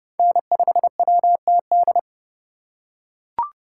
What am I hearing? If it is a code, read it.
Morse code, heard as a keyed tone, about 29 wpm, 700 Hz. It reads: N5WTD